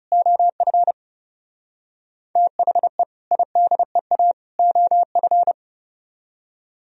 Morse code, heard as a keyed tone, about 36 wpm, 700 Hz. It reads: OF THE IDEA OF
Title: Speed-Racing - Sets of 3 Words 30wpm